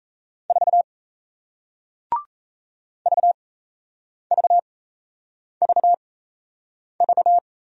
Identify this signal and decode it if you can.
Morse code, heard as a keyed tone, about 38 wpm, spaced as for 8 wpm, 700 Hz. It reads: 4 VVVV